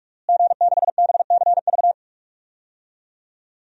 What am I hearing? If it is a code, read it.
Morse code, heard as a keyed tone, about 42 wpm, 700 Hz. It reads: G6BCV
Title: DXCC-100 Call Signs - 5 Characters - Repeated 3x - 45wpm